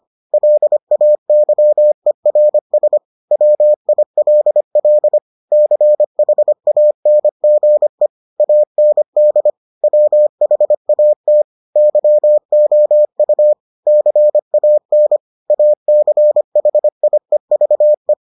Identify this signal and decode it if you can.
Morse code, heard as a keyed tone, about 25 wpm, 600 Hz. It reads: LAYERS WILL CHANGE AND WHAT YOU CAN ACHIEVE